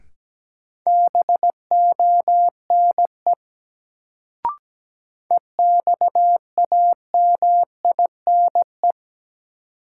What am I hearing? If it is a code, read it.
Morse code, heard as a keyed tone, about 17 wpm, 700 Hz. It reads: BONE EXAMINE
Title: Sentences from Top 1600 Words - New Word Drill - 17wpm